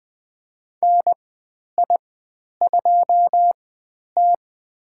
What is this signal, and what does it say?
Morse code, heard as a keyed tone, about 20 wpm, spaced as for 10 wpm, 700 Hz. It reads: NI2T
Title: 102 Lesson 17g - Letter u - Call Signs 20wpm